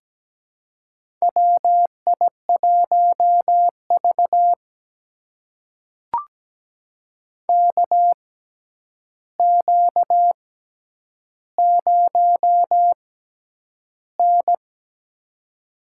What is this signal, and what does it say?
Morse code, heard as a keyed tone, about 17 wpm, spaced as for 9 wpm, 700 Hz. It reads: WI1V KQ0N
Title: USA Call Signs - 6x Character Spacing - 17wpm